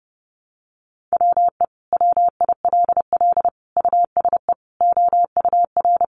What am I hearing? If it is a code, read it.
Morse code, heard as a keyed tone, about 30 wpm, 700 Hz. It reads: WE WILL USE OUR
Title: Sentences from Top 1100 Words 30wpm